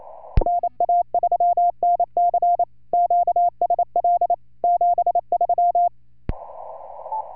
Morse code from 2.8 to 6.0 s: QSL 73